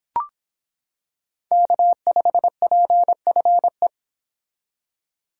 Morse code, heard as a keyed tone, about 26 wpm, 700 Hz. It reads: K5PFE